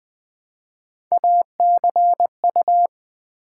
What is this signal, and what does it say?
Morse code, heard as a keyed tone, about 20 wpm, 700 Hz. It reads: ACU
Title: Build A Word - 1 Letter At A Time - 5-Letter Words - 20wpm